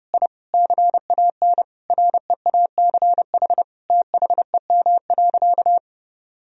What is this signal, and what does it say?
Morse code, heard as a keyed tone, about 32 wpm, 700 Hz. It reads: I CAN REACH THEM.
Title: Sentences from Top 1000 Words - Review Part 2 - 30wpm